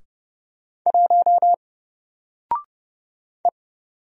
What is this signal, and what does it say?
Morse code, heard as a keyed tone, about 30 wpm, 700 Hz. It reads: 1 E
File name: Single-Letter-Number-30wpm